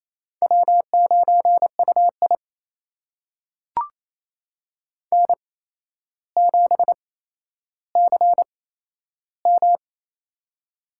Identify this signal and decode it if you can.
Morse code, heard as a keyed tone, about 28 wpm, spaced as for 12 wpm, 700 Hz. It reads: W9UI N7CM
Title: USA Call Signs - 4 Characters - 8x Character Spacing - 28wpm